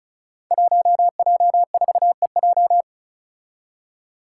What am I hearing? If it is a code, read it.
Morse code, heard as a keyed tone, about 36 wpm, 700 Hz. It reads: 1J4EJ
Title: Five Challenging Letter-Number Combinations 35wpm